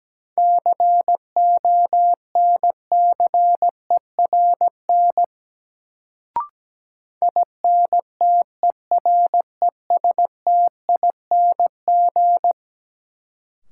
Morse code, heard as a keyed tone, about 17 wpm, 700 Hz. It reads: CONCERN INTERESTING
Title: Sentences from Top 1000 Words - New Word Drill - 17wpm